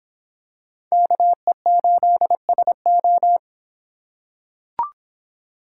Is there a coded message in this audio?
Yes: Morse code from 0.9 to 3.4 s, heard as a keyed tone, about 25 wpm, 700 Hz. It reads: KE8SO